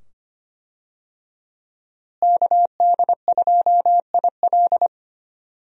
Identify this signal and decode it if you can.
Morse code, heard as a keyed tone, about 25 wpm, 700 Hz. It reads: KD2IL
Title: USA Call Signs - 5 Characters - Repeated 2x - 25wpm